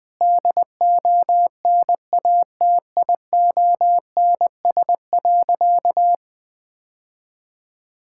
Morse code, heard as a keyed tone, about 20 wpm, 700 Hz. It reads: DONATIONS.
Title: Sentences with 9-Word Phrases - Spoken Beginning - 20wpm